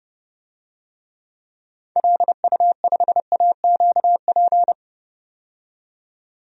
Morse code, heard as a keyed tone, about 30 wpm, 700 Hz. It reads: LU5AQP
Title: DXCC-100 Call Signs - 6 Characters - 30wpm